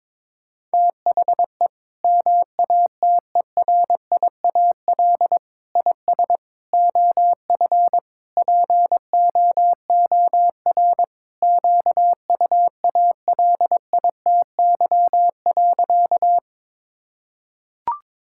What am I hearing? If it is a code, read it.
Morse code, heard as a keyed tone, about 22 wpm, 700 Hz. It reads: THE MATERIAL IS OF POOR QUALITY.
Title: Sentences from Top 1000 Words - Review Part 5 - 22wpm